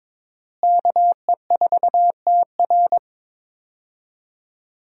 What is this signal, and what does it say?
Morse code, heard as a keyed tone, about 21 wpm, 700 Hz. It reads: KE4TR
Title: USA Call Signs - 5 Characters - Repeated 3x - 22wpm